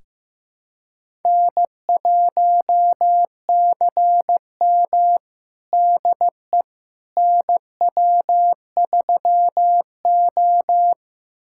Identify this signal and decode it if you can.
Morse code, heard as a keyed tone, about 15 wpm, 700 Hz. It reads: N1CM DE NW3O
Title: USA Callsign DE Callsign - 4 Characters - 15wpm